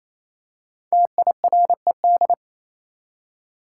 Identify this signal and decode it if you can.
Morse code, heard as a keyed tone, about 28 wpm, 700 Hz. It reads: TIRED
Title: Five Letter Words - 28wpm